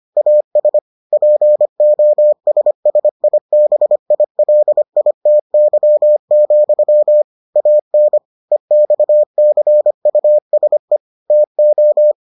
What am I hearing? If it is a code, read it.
Morse code, heard as a keyed tone, about 25 wpm, 600 Hz. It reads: AS POSSIBILITY, AN EXCUSE TO